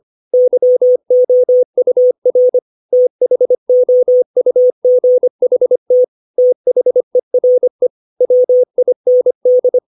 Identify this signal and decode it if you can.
Morse code, heard as a keyed tone, about 25 wpm, 500 Hz. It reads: YOUR THOUGHT THERE WIND
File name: 632 words / 632 Words 25WPM